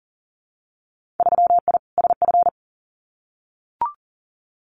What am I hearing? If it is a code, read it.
Morse code, heard as a keyed tone, about 40 wpm, 700 Hz. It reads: 3I SF
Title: Winter Field Day Exchanges 40wpm